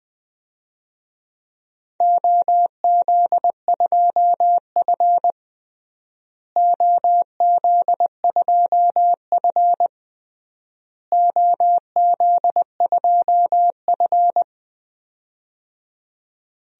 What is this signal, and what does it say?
Morse code, heard as a keyed tone, about 20 wpm, 700 Hz. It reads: OZ2F OZ2F OZ2F
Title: DXCC-100 Call Signs - 4 Characters - Repeated 3x - 20wpm